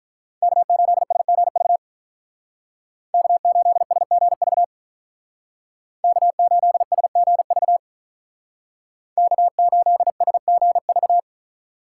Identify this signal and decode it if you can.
Morse code, heard as a keyed tone, about 43 wpm, 700 Hz. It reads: K8SGV K8SGV K8SGV K8SGV
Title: Speed Racing - USA 1x3 Call Signs 35wpm